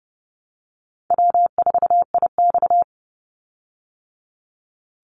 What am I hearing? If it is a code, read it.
Morse code, heard as a keyed tone, about 30 wpm, 700 Hz. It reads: W4IX
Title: Easy Call Signs Repeated 2x - 30wpm